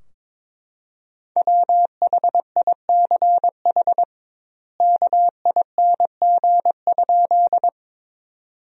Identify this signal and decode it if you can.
Morse code, heard as a keyed tone, about 21 wpm, 700 Hz. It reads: WHICH KING?
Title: 2x Word-Spacing - Sentences from Top 400 Words - 22wpm